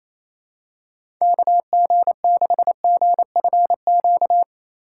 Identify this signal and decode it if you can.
Morse code, heard as a keyed tone, about 28 wpm, 700 Hz. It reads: KG6GFQ
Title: USA Call Signs - 6 Characters - 4x Character Spacing - 28wpm